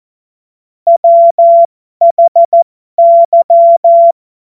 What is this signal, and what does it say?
Morse code, heard as a keyed tone, about 14 wpm, spaced as for 12 wpm, 680 Hz. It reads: WHY
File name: Morsekurs / Witze